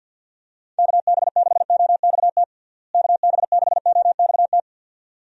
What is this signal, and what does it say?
Morse code, heard as a keyed tone, about 51 wpm, 700 Hz. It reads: KB6OXT KB6OXT
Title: USA Call Signs - 6 Characters - Repeated 2x - 50wpm